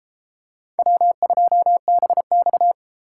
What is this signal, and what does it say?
Morse code, heard as a keyed tone, about 31 wpm, 700 Hz. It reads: W2BX